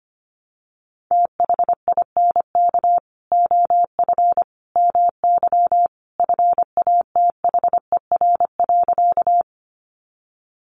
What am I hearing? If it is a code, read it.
Morse code, heard as a keyed tone, about 25 wpm, 700 Hz. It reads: THINK OF MY FATHER.
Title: Sentences from Top 300 Words 25wpm